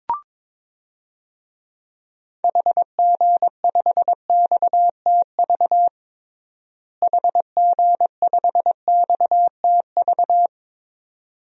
Morse code, heard as a keyed tone, about 22 wpm, 700 Hz. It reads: HG5XTV HG5XTV
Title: DXCC-100 Call Signs - 6 Characters - Repeated 2x - 22wpm